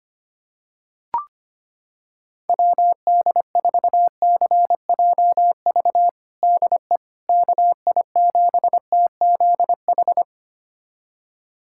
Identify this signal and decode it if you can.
Morse code, heard as a keyed tone, about 25 wpm, 700 Hz. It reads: WD4CJV DE KI7TZH
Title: USA Callsign DE Callsign - 6 Characters - 25wpm